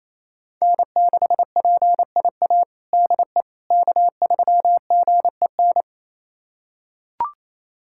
Morse code, heard as a keyed tone, about 28 wpm, 700 Hz. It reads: N6PIA DE K3GEN
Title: USA Callsign DE Callsign - 5 Characters - 28wpm